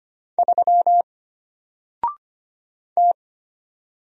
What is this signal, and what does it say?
Morse code, heard as a keyed tone, about 25 wpm, 700 Hz. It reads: 3 T
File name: Single-Letter-Number-25wpm